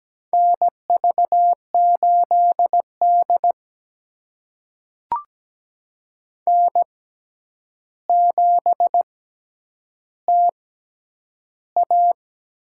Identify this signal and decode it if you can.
Morse code, heard as a keyed tone, about 17 wpm, spaced as for 9 wpm, 700 Hz. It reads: NV8D N7TA
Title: USA Call Signs - 4 Characters - 6x Character Spacing - 17wpm